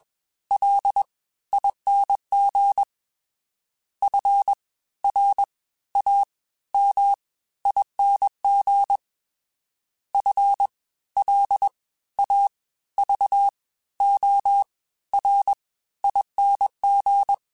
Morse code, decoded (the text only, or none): LING FRAMING FLAVORING